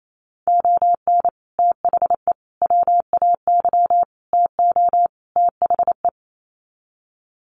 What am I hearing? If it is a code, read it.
Morse code, heard as a keyed tone, about 28 wpm, 700 Hz. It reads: ON THE WAY TO THE
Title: Sets of 5 Words 28wpm